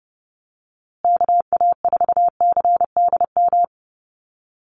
Morse code, heard as a keyed tone, about 30 wpm, 700 Hz. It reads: KA4CDM
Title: USA Call Signs Repeated 2x 30wpm